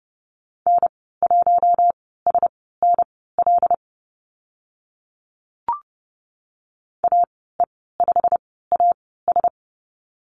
Morse code, heard as a keyed tone, about 30 wpm, spaced as for 17 wpm, 700 Hz. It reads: N1SNL AE5AS
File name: Lesson 15g - Number 5 - Call Signs 30wpm